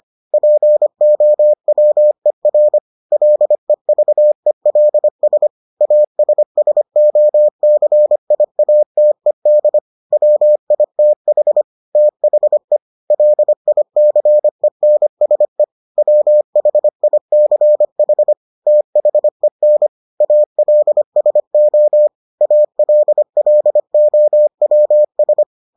Morse code, heard as a keyed tone, about 25 wpm, 600 Hz. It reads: POWER LEVELS ASSOCIATED WITH THE LICENSE WHICH THEN ALSO ALLOWS